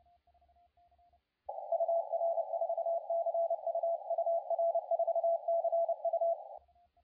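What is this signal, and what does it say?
Morse code, heard as a keyed tone, about 30 wpm, 670 Hz, with radio noise. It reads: UR4CU UR4CU
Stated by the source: CW contest audio recorded at LA4C during the SAC CW contest.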